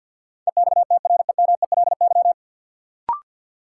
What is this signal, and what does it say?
Morse code, heard as a keyed tone, about 52 wpm, 700 Hz. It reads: EXTREMELY
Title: Sentences from Top 1000 Words - Review Part 7 - New Word Drill - 50wpm